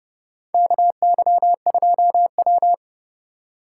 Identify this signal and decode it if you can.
Morse code, heard as a keyed tone, about 30 wpm, 700 Hz. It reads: KY2W
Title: USA Call Signs - 4 Characters - 4x Character Spacing - 30wpm